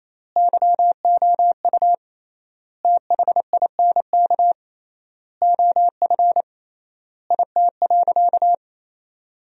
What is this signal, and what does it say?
Morse code, heard as a keyed tone, about 28 wpm, 700 Hz. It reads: YOU THINK OF IT.
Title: 3x Word-Spacing - Sentences from Top 200 Words - 28wpm